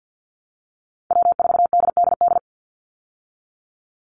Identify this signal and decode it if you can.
Morse code, heard as a keyed tone, about 50 wpm, 700 Hz. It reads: W4DDD
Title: Call Signs Repeated 3x - 50wpm